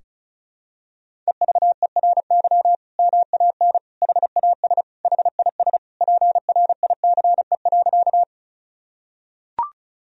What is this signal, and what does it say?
Morse code, heard as a keyed tone, about 34 wpm, 700 Hz. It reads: EVERY MAN HAS HIS PRICE.
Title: Sentences from Top 1000 Words - Review Part 4 - 35wpm